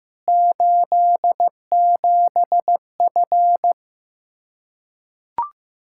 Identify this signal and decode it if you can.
Morse code, heard as a keyed tone, about 15 wpm, 700 Hz. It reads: 87F
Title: Practice QSO - Exchange 2 - Info - 15wpm